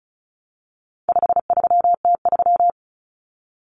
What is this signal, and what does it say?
Morse code, heard as a keyed tone, about 34 wpm, 700 Hz. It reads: 53T3
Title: Common Cut Numbers - 4-Digit 35wpm